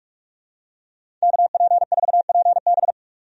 Morse code, heard as a keyed tone, about 46 wpm, 700 Hz. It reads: KP4PB
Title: DXCC-100 Call Signs - 5 Characters - 45wpm